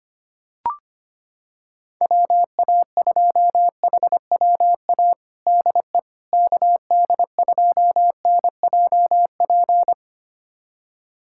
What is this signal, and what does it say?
Morse code, heard as a keyed tone, about 25 wpm, 700 Hz. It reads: WA2HWA DE KD2NJP
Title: USA Callsign DE Callsign - 6 Characters - 25wpm